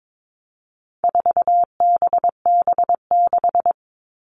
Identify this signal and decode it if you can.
Morse code, heard as a keyed tone, about 22 wpm, 700 Hz. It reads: 4BB6
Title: All Cut Numbers - 4-Digit 22wpm